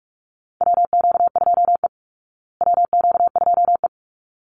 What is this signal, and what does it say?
Morse code, heard as a keyed tone, about 44 wpm, 700 Hz. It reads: WQ2E WQ2E
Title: Call Signs Repeated 3x - 45wpm